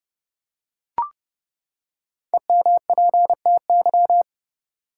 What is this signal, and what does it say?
Morse code, heard as a keyed tone, about 30 wpm, 700 Hz. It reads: EMPTY
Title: Five Letter Words - 30wpm